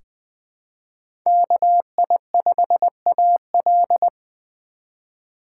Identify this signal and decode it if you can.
Morse code, heard as a keyed tone, about 20 wpm, 700 Hz. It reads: KI5AL
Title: K1USN SST QSO Practice - 20wpm